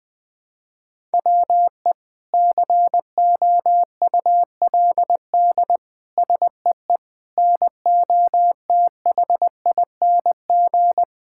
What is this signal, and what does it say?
Morse code, heard as a keyed tone, about 20 wpm, 700 Hz. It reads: WE COULD SEE NOTHING
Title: Sentences from Top 1000 Words - Review Part 5 - 20wpm